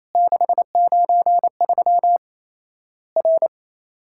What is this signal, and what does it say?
Morse code, heard as a keyed tone, about 28 wpm, 690 Hz. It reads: 693 R
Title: Worked All Europe DX Contest - QTC Exchange - 28wpm